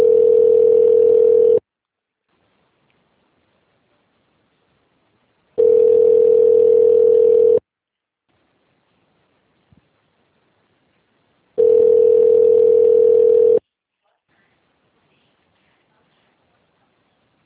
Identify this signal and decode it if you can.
Ringback tone (North America), 440 + 480 Hz.
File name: out- -2010- - - .212